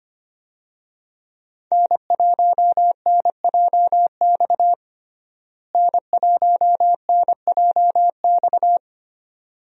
Morse code, heard as a keyed tone, about 25 wpm, 700 Hz. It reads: N1NJX N1NJX
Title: USA Call Signs - 5 Characters - Repeated 2x - 25wpm